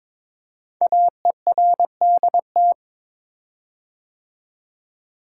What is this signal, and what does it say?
Morse code, heard as a keyed tone, about 22 wpm, 700 Hz. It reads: AERDT
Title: Code Groups - 5 Letters - 22wpm